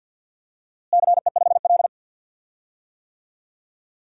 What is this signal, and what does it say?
Morse code, heard as a keyed tone, about 52 wpm, 700 Hz. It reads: CE5L
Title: DXCC-100 Call Signs - 4 Characters - Repeated 3x - 50wpm